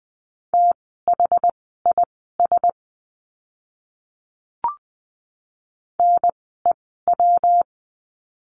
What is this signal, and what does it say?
Morse code, heard as a keyed tone, about 20 wpm, spaced as for 15 wpm, 700 Hz. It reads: THIS NEW
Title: 192 Lesson 31f - Letter v - Words 20wpm